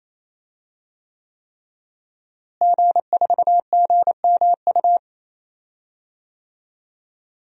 Morse code, heard as a keyed tone, about 28 wpm, 700 Hz. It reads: G4GMU
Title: DXCC-100 Call Signs - 5 Characters - Repeated 3x - 28wpm